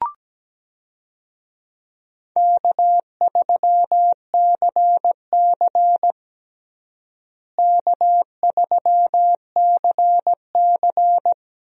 Morse code, heard as a keyed tone, about 17 wpm, 700 Hz. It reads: K3CC K3CC
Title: USA Call Signs - 4 Characters - Repeated 2x - 17wpm